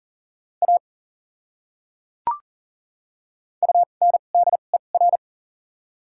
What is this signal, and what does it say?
Morse code, heard as a keyed tone, about 40 wpm, spaced as for 29 wpm, 700 Hz. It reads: A UNDER
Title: 2x Character Spacing - Top 200 Words - 40wpm